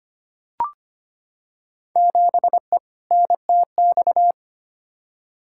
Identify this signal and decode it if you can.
Morse code, heard as a keyed tone, about 25 wpm, 700 Hz. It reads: 7E NTX
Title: Field Day Exchanges Encore - 25wpm